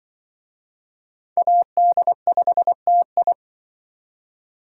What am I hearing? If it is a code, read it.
Morse code, heard as a keyed tone, about 24 wpm, 700 Hz. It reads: AD5TI
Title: Speed Racing - USA 2x2 Call Signs 20wpm